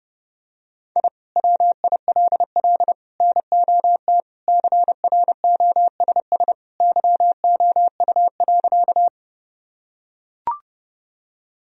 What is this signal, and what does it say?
Morse code, heard as a keyed tone, about 32 wpm, 700 Hz. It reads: I WILL NOT CROSS YOU.
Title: Sentences from Top 1000 Words - Review Part 1 - 30wpm